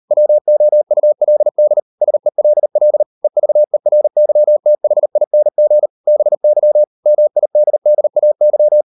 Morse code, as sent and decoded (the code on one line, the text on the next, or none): .-- --- ..- .-.. -.. / ... .-.. .-.. / . ...- . .-. -.-- - .... .. -. --. / -... -.-- / -- .. -.. -.. .- -.--
WOULD SELL EVERYTHING BY MIDDAY